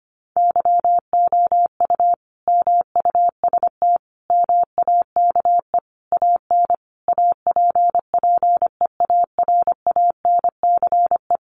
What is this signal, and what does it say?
Morse code, heard as a keyed tone, about 25 wpm, 700 Hz. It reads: YOU MUST MAKE AN APPEARANCE
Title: Sentences from Top 1100 Words 25wpm